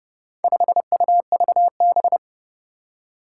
Morse code, heard as a keyed tone, about 30 wpm, 700 Hz. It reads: 5UVB
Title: Four Challenging Letter-Number Combinations 30wpm